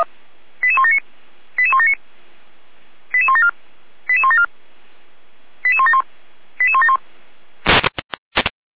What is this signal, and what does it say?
Five-tone selective call (ZVEI), heard as quick runs of five tones, tones of about 70 ms. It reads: 88179 88179 88174 88174 88171 88171